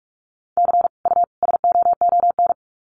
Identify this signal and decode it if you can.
Morse code, heard as a keyed tone, about 44 wpm, 700 Hz. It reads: C U SOON
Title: Common QSO Phrases 45wpm